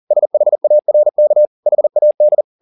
Morse code, heard as a keyed tone, about 40 wpm, 600 Hz. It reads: SHARK HAD
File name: the old man and the sea 2 40wpm